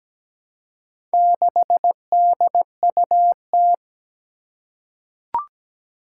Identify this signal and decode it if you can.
Morse code, heard as a keyed tone, about 17 wpm, 700 Hz. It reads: 6DUT
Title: Four Challenging Letter-Number Combinations 17wpm